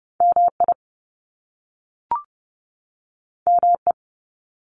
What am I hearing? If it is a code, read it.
Morse code, heard as a keyed tone, about 30 wpm, 700 Hz. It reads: MI ME
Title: US States 30wpm